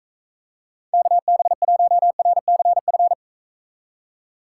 Morse code, heard as a keyed tone, about 44 wpm, 700 Hz. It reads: KB1RCF